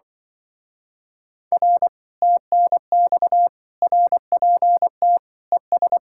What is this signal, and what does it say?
Morse code, heard as a keyed tone, about 24 wpm, 700 Hz. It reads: R TNX RPT ES
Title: Speed-Racing Common QSO Phrases Encore - 20wpm